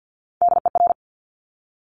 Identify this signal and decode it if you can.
Morse code, heard as a keyed tone, about 50 wpm, 690 Hz. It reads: DER
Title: Top Three Letter Combinations in Words 50wpm